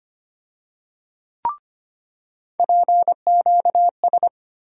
Morse code, heard as a keyed tone, about 25 wpm, 700 Hz. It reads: PQS
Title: USA Call Sign Postfixes - 3 Letters 25wpm